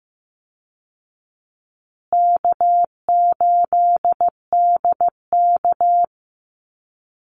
Morse code, heard as a keyed tone, about 15 wpm, 700 Hz. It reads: K8DK
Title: Easy Call Signs Repeated 2x - 15wpm